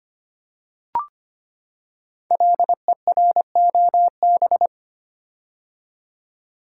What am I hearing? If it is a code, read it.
Morse code, heard as a keyed tone, about 25 wpm, 700 Hz. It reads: LEROB
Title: Code Groups - 5 Letters - 25wpm